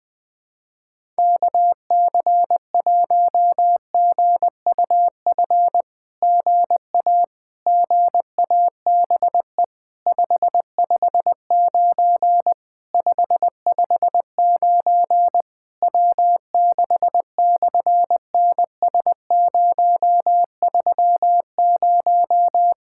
Morse code, heard as a keyed tone, about 20 wpm, 700 Hz. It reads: KC1GUF GA GABE 559 559 W6/NS030
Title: SOTA QSO Practice 20wpm